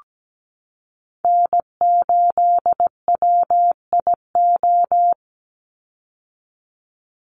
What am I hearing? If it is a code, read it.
Morse code, heard as a keyed tone, about 17 wpm, 700 Hz. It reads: N8WIO
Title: USA Call Signs 17wpm